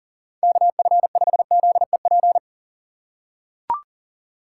Morse code, heard as a keyed tone, about 37 wpm, 700 Hz. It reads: KF5ZEP
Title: USA Call Signs - 6 Characters - 3x Character Spacing - 40wpm